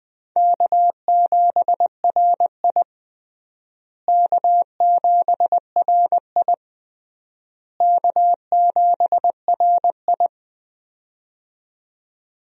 Morse code, heard as a keyed tone, about 21 wpm, 700 Hz. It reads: K7RI K7RI K7RI
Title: USA Call Signs - 4 Characters - Repeated 3x - 20wpm